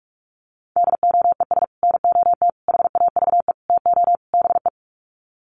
Morse code, heard as a keyed tone, about 44 wpm, 700 Hz. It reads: DOES NOT HAVE TO BE
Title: Sets of 5 Words 45wpm